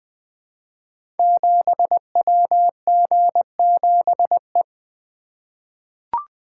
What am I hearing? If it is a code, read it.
Morse code, heard as a keyed tone, about 20 wpm, 700 Hz. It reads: 7WG7E
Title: Code Groups - 5 Letter:Numbers - 20wpm